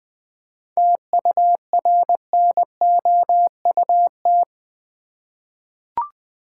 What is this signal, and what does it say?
Morse code, heard as a keyed tone, about 20 wpm, 700 Hz. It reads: TURNOUT
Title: 7 Letter Words - First 3 Letters Given - 20wpm